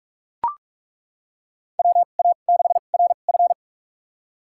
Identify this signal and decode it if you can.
Morse code, heard as a keyed tone, about 42 wpm, spaced as for 31 wpm, 700 Hz. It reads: WA6RF
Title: USA Call Signs - 5 Characters - 2x Character Spacing - 45wpm